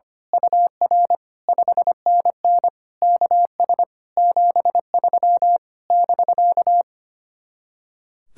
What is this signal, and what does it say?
Morse code, heard as a keyed tone, about 24 wpm, 700 Hz. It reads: UR 5NN KS 73 <BK>
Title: Speed-Racing Common QSO Phrases - 25wpm